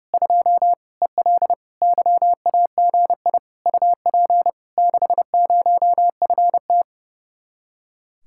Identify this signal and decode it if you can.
Morse code, heard as a keyed tone, about 33 wpm, 700 Hz. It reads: 2 EL YAGI UP 60FT
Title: Speed-Racing Common QSO Phrases - 30wpm